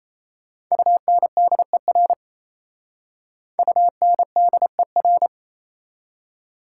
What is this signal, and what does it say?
Morse code, heard as a keyed tone, about 35 wpm, 700 Hz. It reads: UNDER UNDER
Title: Speed-Racing Prefixes and Suffixes 28wpm